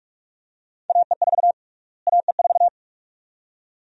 Morse code, heard as a keyed tone, about 44 wpm, 700 Hz. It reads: AE4 AE4
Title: USA Call Sign Prefixes - Repeated 2x - 45wpm